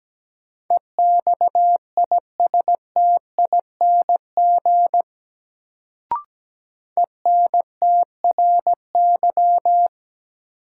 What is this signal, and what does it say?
Morse code, heard as a keyed tone, about 17 wpm, 700 Hz. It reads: EXISTING ENTRY
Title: Sentences from Top 1500 Words - New Word Drill - 17wpm